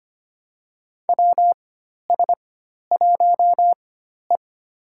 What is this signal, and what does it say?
Morse code, heard as a keyed tone, about 25 wpm, spaced as for 12 wpm, 700 Hz. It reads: WS1E
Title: USA Call Signs - 4 Characters - 4x Character Spacing - 25wpm